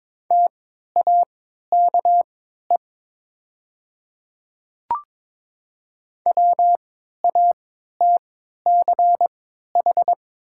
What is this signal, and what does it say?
Morse code, heard as a keyed tone, about 22 wpm, spaced as for 12 wpm, 700 Hz. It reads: TAKE WATCH
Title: 3x Character Spacing - Top 500 Words - 22wpm